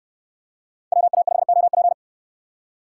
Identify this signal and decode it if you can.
Morse code, heard as a keyed tone, about 69 wpm, 700 Hz. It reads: WA5OP